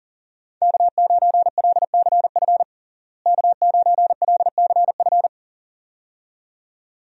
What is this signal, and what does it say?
Morse code, heard as a keyed tone, about 40 wpm, 700 Hz. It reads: K9LCF K9LCF
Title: USA Call Signs - 5 Characters - Repeated 2x - 40wpm